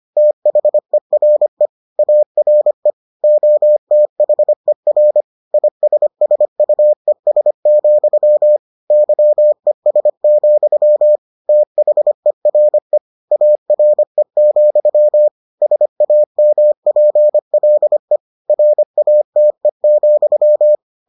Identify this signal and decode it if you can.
Morse code, heard as a keyed tone, about 25 wpm, 600 Hz. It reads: THERE ARE OTHER ISSUES, YES, THERE ARE, SAMPLE RATE,